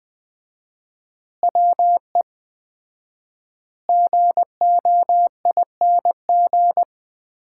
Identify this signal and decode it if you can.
Morse code, heard as a keyed tone, about 20 wpm, 700 Hz. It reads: WE GOING
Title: 4x Word-Spacing - Sets of 5 Words - 20wpm